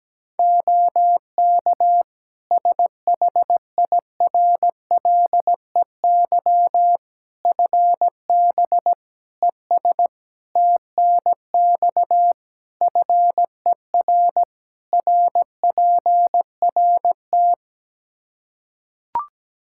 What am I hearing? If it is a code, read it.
Morse code, heard as a keyed tone, about 17 wpm, 700 Hz. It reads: OK SHIRLEY FB ES TNX FER RPRT
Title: Practice QSO - Exchange 2 - Info - 17wpm